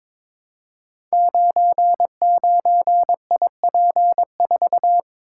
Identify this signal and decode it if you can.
Morse code, heard as a keyed tone, about 22 wpm, 700 Hz. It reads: 99IP4
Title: Code Groups - 5 Letter:Numbers - 22wpm